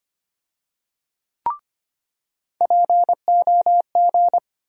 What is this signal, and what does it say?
Morse code, heard as a keyed tone, about 25 wpm, 700 Hz. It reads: POG
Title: USA Call Sign Postfixes - 3 Letters 25wpm